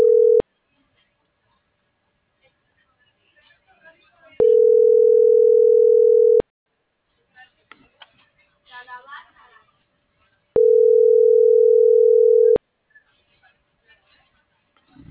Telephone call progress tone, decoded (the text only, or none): ringback tone